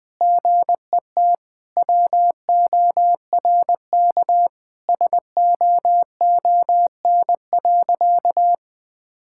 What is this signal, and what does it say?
Morse code, heard as a keyed tone, about 20 wpm, 700 Hz. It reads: GET WORK SOON.
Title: Sentences from Top 1000 Words - Review Part 6 - 20wpm